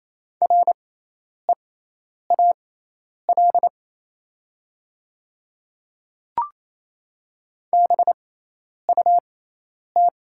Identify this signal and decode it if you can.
Morse code, heard as a keyed tone, about 28 wpm, spaced as for 10 wpm, 700 Hz. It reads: REAL BUT
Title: 6x Character Spacing - Top 300 Words - 28wpm